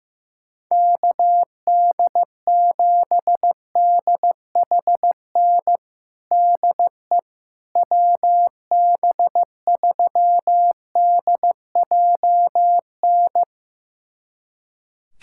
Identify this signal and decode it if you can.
Morse code, heard as a keyed tone, about 15 wpm, 700 Hz. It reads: KD7DHN DE WB3DJN
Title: USA Callsign DE Callsign 15wpm